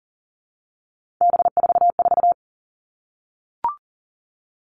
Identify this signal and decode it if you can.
Morse code, heard as a keyed tone, about 40 wpm, 700 Hz. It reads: B44
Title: All Cut Numbers - 3-Digit 40wpm